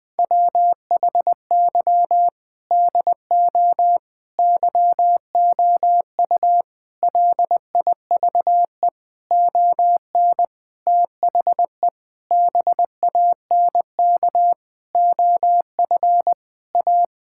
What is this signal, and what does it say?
Morse code, heard as a keyed tone, about 21 wpm, 700 Hz. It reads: WHY DO YOU LIVE ON THE BANK OF A
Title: Sentences from Top 1000 Words - Review Part 6 - 20wpm